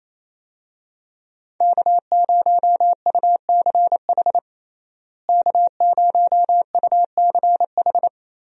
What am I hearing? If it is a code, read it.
Morse code, heard as a keyed tone, about 28 wpm, 700 Hz. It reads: K0UCH K0UCH
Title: USA Call Signs - 5 Characters - Repeated 3x - 28wpm